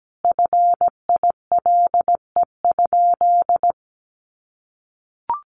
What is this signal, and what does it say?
Morse code, heard as a keyed tone, about 17 wpm, 700 Hz. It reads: FILE?
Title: Sentences from Top 1200 Words 17wpm